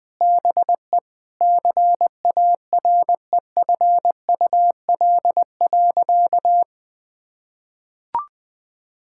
Morse code, heard as a keyed tone, about 20 wpm, 700 Hz. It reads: BE CAREFUL.
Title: Sentences with 5-Word Phrases - Spoken Beginning - 20wpm